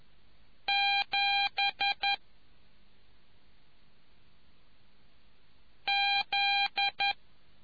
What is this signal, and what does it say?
Morse code, heard as a keyed tone, about 11 wpm, 2400 Hz. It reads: 7 Z